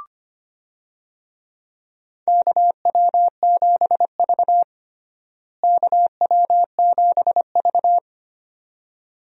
Morse code, heard as a keyed tone, about 25 wpm, 700 Hz. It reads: KW7V KW7V
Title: USA Call Signs - 4 Characters - Repeated 2x - 25wpm